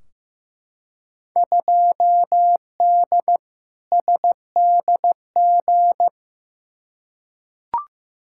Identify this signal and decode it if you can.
Morse code, heard as a keyed tone, about 16 wpm, 700 Hz. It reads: 2D SDG
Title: Field Day Exchanges Encore - 15wpm